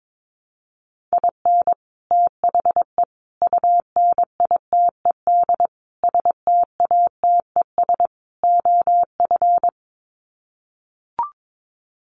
Morse code, heard as a keyed tone, about 22 wpm, 700 Hz. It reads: IN THE UNITED STATES OF
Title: Sets of 5 Words Encore2 22wpm